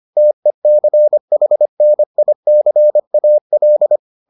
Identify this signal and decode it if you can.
Morse code, heard as a keyed tone, about 25 wpm, 600 Hz. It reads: TECHNICAL